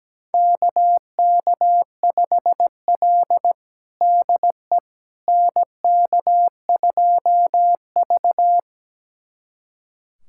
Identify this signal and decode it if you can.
Morse code, heard as a keyed tone, about 17 wpm, 700 Hz. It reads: KK5L DE NK2V
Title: USA Callsign DE Callsign - 4 Characters - 17wpm